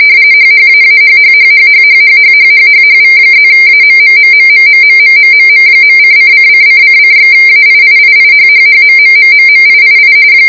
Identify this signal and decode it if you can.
RTTY, about 45.45 baud. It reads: HHAAHHHHHHHHHHHHV+AMMMMMMMMMMM.:::::::/AIIIIHHHMMMMMIII